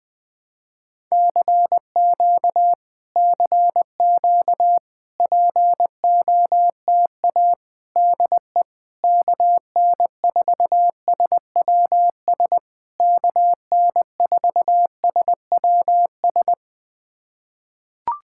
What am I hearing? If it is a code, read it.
Morse code, heard as a keyed tone, about 20 wpm, 700 Hz. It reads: CQ CQ POTA DE KN4SWS KN4SWS
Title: POTA QSO Practice - 20wpm